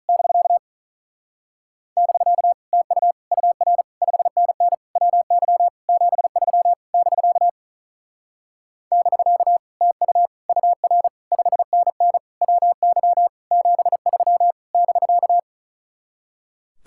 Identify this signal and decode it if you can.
Morse code, heard as a keyed tone, about 44 wpm, 700 Hz. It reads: <BK> <BK> TU UR 5NN WY 73 <BK> <BK> TU UR 5NN WY 73 <BK>